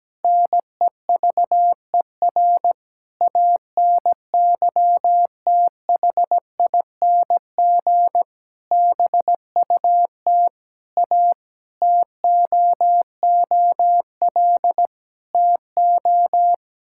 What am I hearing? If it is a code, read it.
Morse code, heard as a keyed tone, about 17 wpm, 700 Hz. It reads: NEVER ANYTHING BUT A TOOL TO
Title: Sentences from Top 1000 Words - Review Part 6 - 17wpm